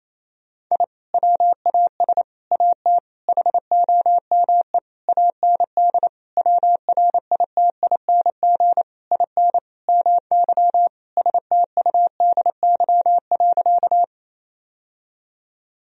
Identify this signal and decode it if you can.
Morse code, heard as a keyed tone, about 28 wpm, 700 Hz. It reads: I WAS AT HOME AND WRITING IN MY STUDY.
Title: Sentences from Top 1000 Words - Review Part 1 - 28wpm